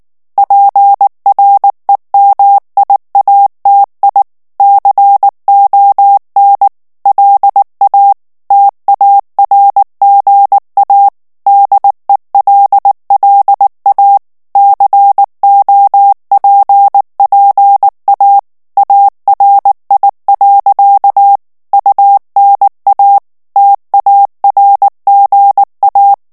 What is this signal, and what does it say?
Morse code, heard as a keyed tone, about 19 wpm, 800 Hz. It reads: PREMIATI CON LA TARGA DELLA COPPA ARI. UNA TARGA
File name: morse mp3 / ITALIEN texte clair 2 vitesse 18 mots minute